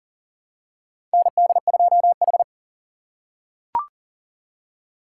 Morse code, heard as a keyed tone, about 40 wpm, 700 Hz. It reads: ND2H
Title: USA Call Signs - 4 Characters - 2x Character Spacing - 40wpm